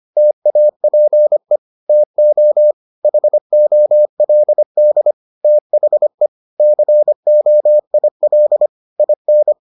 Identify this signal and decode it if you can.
Morse code, heard as a keyed tone, about 25 wpm, 600 Hz. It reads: TAPE TO HOLD THE COIL IN